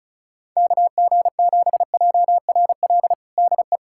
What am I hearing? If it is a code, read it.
Morse code, heard as a keyed tone, about 35 wpm, 700 Hz. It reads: KG7JRL DE
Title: USA Callsign DE Callsign - 6 Characters - 35wpm